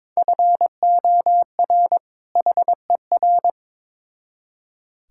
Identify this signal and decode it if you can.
Morse code, heard as a keyed tone, about 22 wpm, 700 Hz. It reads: FOR HER
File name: Build A Phrase - 2-Word Phrases - 22wpm